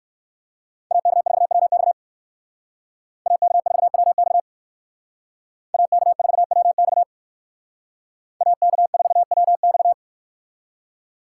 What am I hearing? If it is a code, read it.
Morse code, heard as a keyed tone, about 57 wpm, 700 Hz. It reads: AK4WX AK4WX AK4WX AK4WX